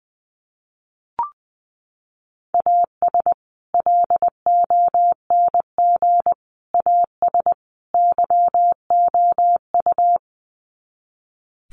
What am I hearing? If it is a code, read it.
Morse code, heard as a keyed tone, about 20 wpm, 700 Hz. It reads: AS LONG AS YOU
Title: Sets of 4 Words 20wpm